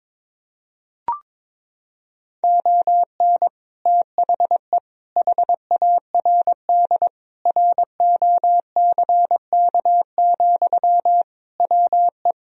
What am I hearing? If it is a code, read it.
Morse code, heard as a keyed tone, about 22 wpm, 700 Hz. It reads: ON THE HARD ROCK, WE
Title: Sentences from Top 1000 Words - Review Part 2 - 22wpm